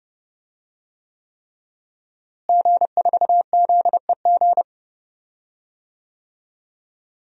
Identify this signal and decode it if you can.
Morse code, heard as a keyed tone, about 31 wpm, 700 Hz. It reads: G4ZEG
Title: DXCC-100 Call Signs - 5 Characters - Repeated 3x - 30wpm